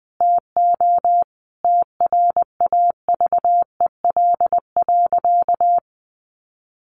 Morse code, heard as a keyed tone, about 20 wpm, 700 Hz. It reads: TO TRAVEL.
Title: A Sound Of Thunder 20wpm